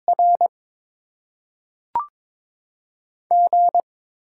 Morse code, heard as a keyed tone, about 22 wpm, 700 Hz. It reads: R G
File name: Top 20 QSO Elements - 6x Character Spacing - 22wpm